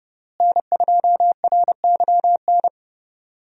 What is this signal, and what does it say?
Morse code, heard as a keyed tone, about 32 wpm, 700 Hz. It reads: N2RYN
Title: USA Call Signs - 5 Characters - Repeated 3x - 30wpm